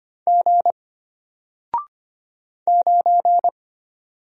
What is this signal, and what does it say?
Morse code, heard as a keyed tone, about 25 wpm, 700 Hz. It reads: G 9
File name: Single-Letter-Number-25wpm